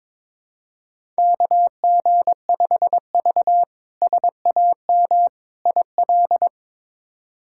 Morse code, heard as a keyed tone, about 22 wpm, 700 Hz. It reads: KG5V SAM IL
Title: K1USN SST QSO Practice - 22wpm